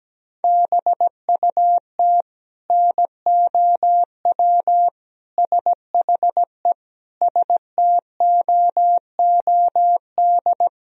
Morse code, heard as a keyed tone, about 17 wpm, 700 Hz. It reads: BUT NOW SHE STOOD
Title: Sentences from Top 1000 Words - Review Part 2 - 17wpm